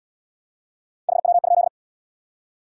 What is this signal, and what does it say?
Morse code, heard as a keyed tone, about 69 wpm, 700 Hz. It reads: HW?